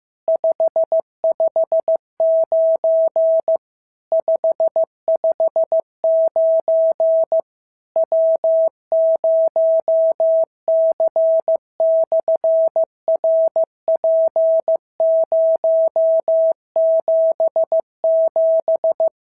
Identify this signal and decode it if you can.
Morse code, heard as a keyed tone, about 15 wpm, 650 Hz. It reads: 559 559 W0C/RP077
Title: SOTA QSO Practice 15wpm